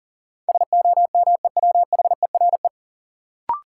Morse code, heard as a keyed tone, about 40 wpm, 700 Hz. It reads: SOMEWHERE
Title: Sentences from Top 1400 Words - New Word Drill Alt - 40wpm